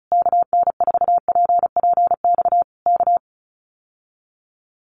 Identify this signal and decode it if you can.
Morse code, heard as a keyed tone, about 34 wpm, 700 Hz. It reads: KN4PPX K
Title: Common QSO Phrases Encore 35wpm